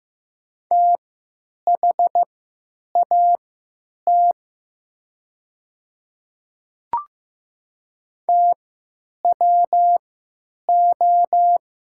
Morse code, heard as a keyed tone, about 15 wpm, spaced as for 9 wpm, 700 Hz. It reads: THAT TWO
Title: 3x Character Spacing - Top 300 Words - 15wpm